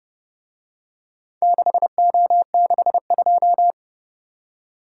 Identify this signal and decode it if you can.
Morse code, heard as a keyed tone, about 32 wpm, 700 Hz. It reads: 6O62